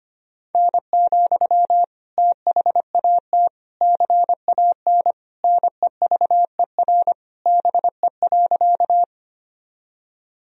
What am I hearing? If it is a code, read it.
Morse code, heard as a keyed tone, about 25 wpm, 700 Hz. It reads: N, THAT CAN NEVER BE.
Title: Sentences from Top 300 Words and Abbreviations 25wpm